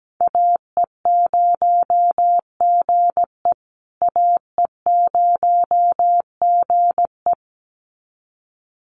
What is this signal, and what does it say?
Morse code, heard as a keyed tone, about 17 wpm, 700 Hz. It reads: AE0GE AE0GE
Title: USA Call Signs Repeated 2x 17wpm